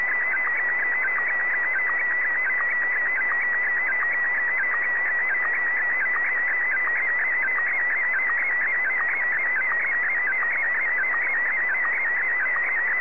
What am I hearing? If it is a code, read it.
SSTV (slow-scan television), PD 90.